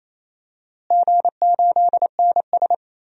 Morse code, heard as a keyed tone, about 28 wpm, 700 Hz. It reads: G8NS
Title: DXCC-100 Call Signs - 4 Characters - Repeated 2x - 28wpm